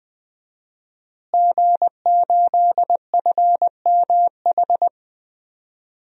Morse code, heard as a keyed tone, about 21 wpm, 700 Hz. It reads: G8FMH
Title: DXCC-100 Call Signs - 5 Characters - Repeated 3x - 20wpm